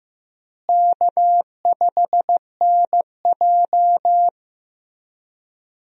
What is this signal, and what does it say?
Morse code, heard as a keyed tone, about 15 wpm, 700 Hz. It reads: K5NJ
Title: USA Call Signs - 4 Characters - Repeated 3x - 15wpm